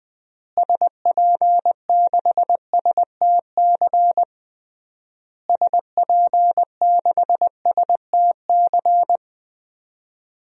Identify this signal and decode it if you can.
Morse code, heard as a keyed tone, about 20 wpm, 700 Hz. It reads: SP6STC SP6STC
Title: DXCC-100 Call Signs - 6 Characters - Repeated 3x - 20wpm